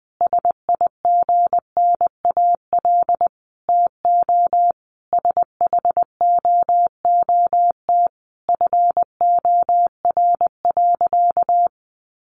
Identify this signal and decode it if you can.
Morse code, heard as a keyed tone, about 20 wpm, 700 Hz. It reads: SIGNAL TO SHOOT FOR.
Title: Sentences from Top 1400 Words 20wpm